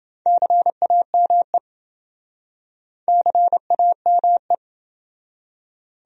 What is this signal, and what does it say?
Morse code, heard as a keyed tone, about 29 wpm, 700 Hz. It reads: CAME CAME